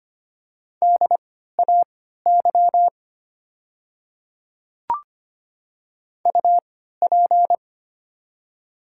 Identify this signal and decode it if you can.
Morse code, heard as a keyed tone, about 25 wpm, spaced as for 14 wpm, 700 Hz. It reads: DAY UP
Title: 3x Character Spacing - Top 100 Words - 25wpm